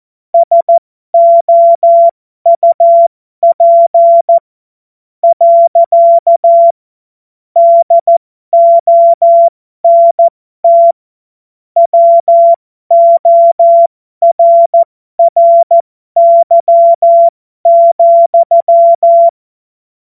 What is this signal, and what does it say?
Morse code, heard as a keyed tone, about 14 wpm, spaced as for 11 wpm, 680 Hz. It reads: SOUP. DONT WORRY,
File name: Morsekurs / Witze